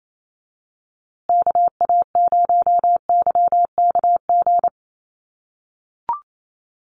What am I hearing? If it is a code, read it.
Morse code, heard as a keyed tone, about 28 wpm, 700 Hz. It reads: KA0YKG
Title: USA Call Signs Repeated 3x 28wpm